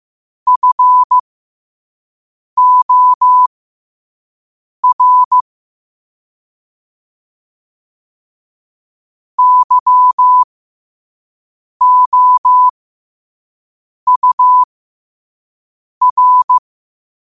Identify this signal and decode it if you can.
Morse code, heard as a keyed tone, about 15 wpm, spaced as for 5 wpm, 1000 Hz. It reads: FOR YOUR
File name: qotdfiles / 5wpm